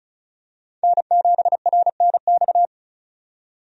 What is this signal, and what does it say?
Morse code, heard as a keyed tone, about 36 wpm, 700 Hz. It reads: N7RNX
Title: USA Call Signs - 5 Characters - 35wpm